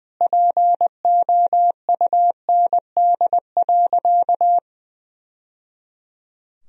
Morse code, heard as a keyed tone, about 20 wpm, 700 Hz. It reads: POUND.
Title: Sentences from Top 1000 Words - Review Part 2 - 20wpm